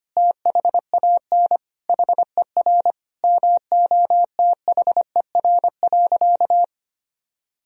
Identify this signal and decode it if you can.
Morse code, heard as a keyed tone, about 25 wpm, 700 Hz. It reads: THAN HER MOTHER.
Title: Sentences from Top 400 Words and Abbreviations 25wpm